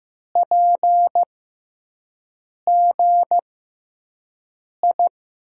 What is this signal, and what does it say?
Morse code, heard as a keyed tone, about 15 wpm, spaced as for 5 wpm, 700 Hz. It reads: PGI
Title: USA Call Signs - 6 Characters - 6x Character Spacing - 15wpm